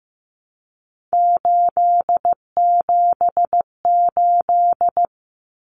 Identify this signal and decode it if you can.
Morse code, heard as a keyed tone, about 15 wpm, 700 Hz. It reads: 878
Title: Common Cut Numbers - 3-Digit 15wpm